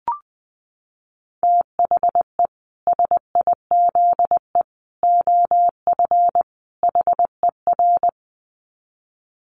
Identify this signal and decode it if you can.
Morse code, heard as a keyed tone, about 20 wpm, 700 Hz. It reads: THE SIZE OF HER
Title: Dit Challenge 20wpm